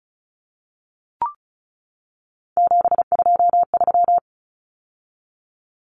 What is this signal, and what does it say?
Morse code, heard as a keyed tone, about 34 wpm, 700 Hz. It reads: 723
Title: Numbers - 3-Digit 35wpm